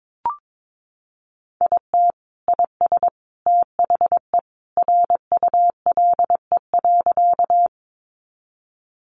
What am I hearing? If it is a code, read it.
Morse code, heard as a keyed tone, about 22 wpm, 700 Hz. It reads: IT IS THE RULE.
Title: Sentences from Top 400 Words 22wpm